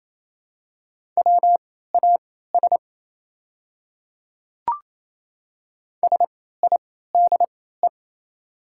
Morse code, heard as a keyed tone, about 27 wpm, spaced as for 16 wpm, 700 Hz. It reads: WAS SIDE
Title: 3x Character Spacing - Top 100 Words - 28wpm